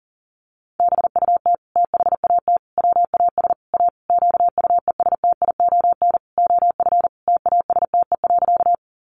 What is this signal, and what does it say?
Morse code, heard as a keyed tone, about 40 wpm, 700 Hz. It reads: BUT THAT WAS A QUESTION OF TASTE.
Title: Sentences from Top 1100 Words 40wpm